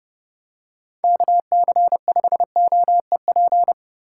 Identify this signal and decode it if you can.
Morse code, heard as a keyed tone, about 29 wpm, 700 Hz. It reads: KC5OEP